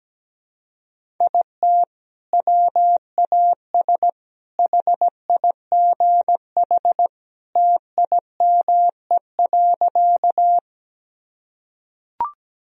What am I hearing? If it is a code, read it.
Morse code, heard as a keyed tone, about 17 wpm, 700 Hz. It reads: IT WAS HIGH TIME.
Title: Sentences from Top 1000 Words - Review Part 1 - 17wpm